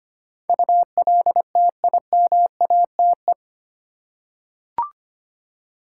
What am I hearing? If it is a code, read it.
Morse code, heard as a keyed tone, about 25 wpm, 700 Hz. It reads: ULTIMATE
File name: 8 Letter Words - First 3 Letters Given - 25wpm